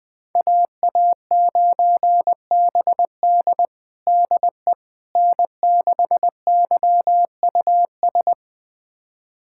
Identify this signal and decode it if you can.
Morse code, heard as a keyed tone, about 20 wpm, 700 Hz. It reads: AA9BD DE N6YUS
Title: USA Callsign DE Callsign 20wpm